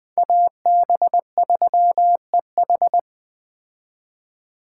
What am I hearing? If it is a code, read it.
Morse code, heard as a keyed tone, about 20 wpm, 700 Hz. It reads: AB3EH
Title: Speed Racing - USA 2x2 Call Signs 17wpm